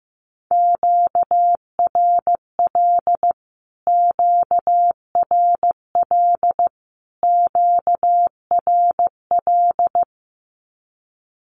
Morse code, heard as a keyed tone, about 15 wpm, 700 Hz. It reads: QRL QRL QRL
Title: Q-Codes Repeated 3x 15wpm